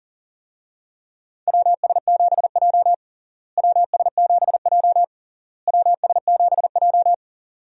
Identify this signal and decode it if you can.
Morse code, heard as a keyed tone, about 40 wpm, 700 Hz. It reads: WS7J WS7J WS7J
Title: USA Call Signs - 4 Characters - Repeated 3x - 40wpm